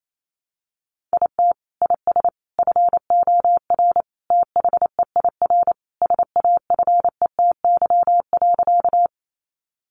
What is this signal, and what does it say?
Morse code, heard as a keyed tone, about 28 wpm, 700 Hz. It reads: IT IS FOR THEIR SAFETY.
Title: Sentences from Top 700 Words 28wpm